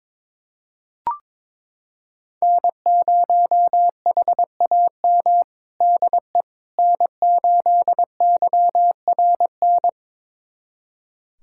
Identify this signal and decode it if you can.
Morse code, heard as a keyed tone, about 22 wpm, 700 Hz. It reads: N0HAM DE N8YRN
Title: USA Callsign DE Callsign - 5 Characters - 22wpm